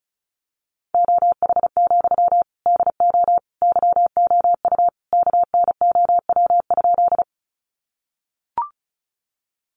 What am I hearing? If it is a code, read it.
Morse code, heard as a keyed tone, about 34 wpm, 700 Hz. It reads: OH, DO YOU KNOW?
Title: Sentences from Top 500 Words 35wpm